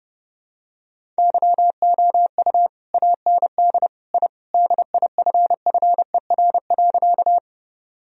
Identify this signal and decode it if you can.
Morse code, heard as a keyed tone, about 30 wpm, 700 Hz. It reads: YOU AND I DIFFER.
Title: Sentences from Top 1000 Words - Review Part 1 - 30wpm